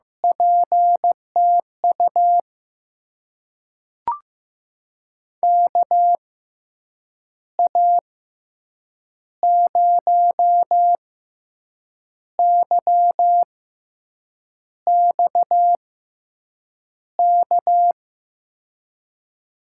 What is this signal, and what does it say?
Morse code, heard as a keyed tone, about 15 wpm, spaced as for 6 wpm, 700 Hz. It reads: PTU KA0YXK
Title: USA Call Signs - 6 Characters - 6x Character Spacing - 15wpm